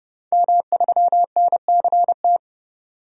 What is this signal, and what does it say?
Morse code, heard as a keyed tone, about 30 wpm, 700 Hz. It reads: M3NCT
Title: DXCC-100 Call Signs - 5 Characters - 30wpm